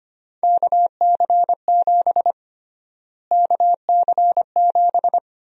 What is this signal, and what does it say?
Morse code, heard as a keyed tone, about 25 wpm, 700 Hz. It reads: KC7 KC7
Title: USA Call Sign Prefixes - Repeated 2x - 25wpm